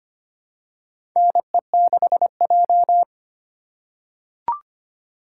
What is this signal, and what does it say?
Morse code, heard as a keyed tone, about 25 wpm, 700 Hz. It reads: NE6J
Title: USA Call Signs - 4 Characters - 2x Character Spacing - 25wpm